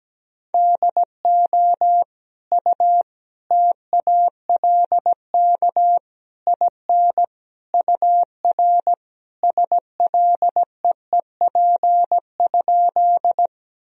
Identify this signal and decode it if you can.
Morse code, heard as a keyed tone, about 17 wpm, 700 Hz. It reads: DO U TALK IN UR SLEEP?
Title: Sentences from Top 500 Words and Abbreviations 17wpm